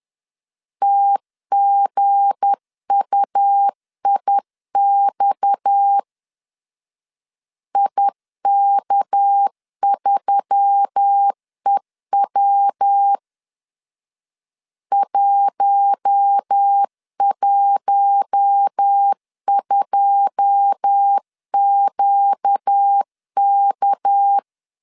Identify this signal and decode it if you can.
Morse code, heard as a keyed tone, about 10 wpm, 800 Hz. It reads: TGUIX IK3EW 112QK
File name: LEZIONI / 9 2 10wpm 2736